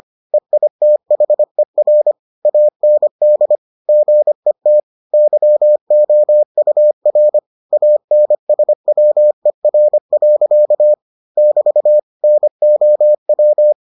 Morse code, heard as a keyed tone, about 25 wpm, 600 Hz. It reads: EITHER AND GET YOUR ANSWER. <BT> NOW